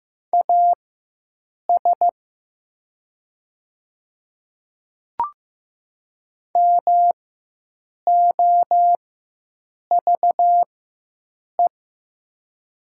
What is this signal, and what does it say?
Morse code, heard as a keyed tone, about 15 wpm, spaced as for 7 wpm, 700 Hz. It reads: AS MOVE
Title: 4x Character Spacing - Top 200 Words - 15wpm